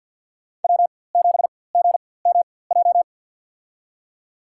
Morse code, heard as a keyed tone, about 52 wpm, spaced as for 23 wpm, 700 Hz. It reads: W7GMJ
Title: USA Call Signs - 5 Characters - 4x Character Spacing - 50wpm